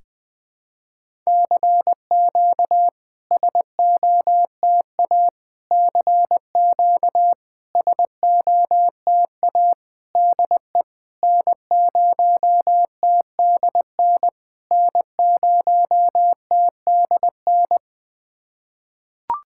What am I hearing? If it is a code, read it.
Morse code, heard as a keyed tone, about 21 wpm, 700 Hz. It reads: CQ SOTA CQ SOTA DE N0TDN N0TDN
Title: SOTA QSO Practice 20wpm